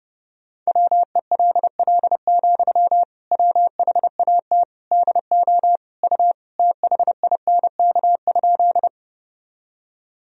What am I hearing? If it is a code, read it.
Morse code, heard as a keyed tone, about 32 wpm, 700 Hz. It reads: WELL, WHAT DO U THINK?
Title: Sentences from Top 200 Words and Abbreviations 30wpm